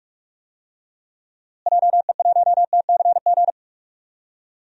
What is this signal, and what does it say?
Morse code, heard as a keyed tone, about 46 wpm, 700 Hz. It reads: JE1TCG
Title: DXCC-100 Call Signs - 6 Characters - Repeated 3x - 45wpm